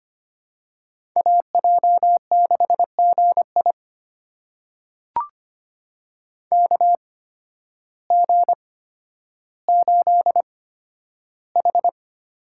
Morse code, heard as a keyed tone, about 25 wpm, spaced as for 12 wpm, 700 Hz. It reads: AJ6GI KG8H
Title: USA Call Signs - 5 Characters - 8x Character Spacing - 25wpm